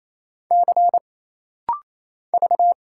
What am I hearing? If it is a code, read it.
Morse code, heard as a keyed tone, about 27 wpm, 700 Hz. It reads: C V
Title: Single Letter-Number Alt - Rapid-Fire 28wpm